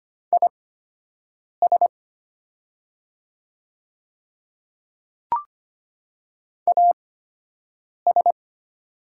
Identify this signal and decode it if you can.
Morse code, heard as a keyed tone, about 25 wpm, spaced as for 7 wpm, 690 Hz. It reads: IS AS
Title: 8x Character Spacing - Top 100 Words - 25wpm